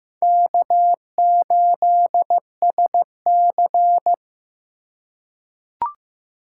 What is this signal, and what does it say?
Morse code, heard as a keyed tone, about 15 wpm, 700 Hz. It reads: K8SC
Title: USA Call Signs - 4 Characters - 2x Character Spacing - 15wpm